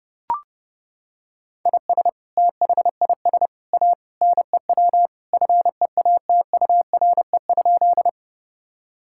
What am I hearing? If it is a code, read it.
Morse code, heard as a keyed tone, about 30 wpm, 700 Hz. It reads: IS THIS A NEW FEATURE?
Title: Sentences from Top 1000 Words - Review Part 5 - 30wpm